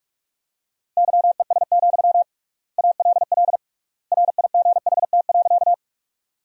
Morse code, heard as a keyed tone, about 46 wpm, 700 Hz. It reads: YES, ALL RIGHT.
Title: 3x Word-Spacing - Sentences from Top 500 Words - 45wpm